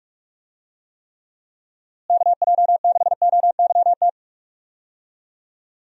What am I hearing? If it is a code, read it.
Morse code, heard as a keyed tone, about 46 wpm, 700 Hz. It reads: KJ6OYT
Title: USA Call Signs - 6 Characters - Repeated 2x - 45wpm